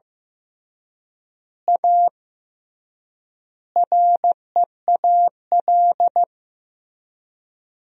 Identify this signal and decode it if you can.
Morse code, heard as a keyed tone, about 15 wpm, 700 Hz. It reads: A REAL
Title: 3x Word-Spacing - Sentences from Top 500 Words - 15wpm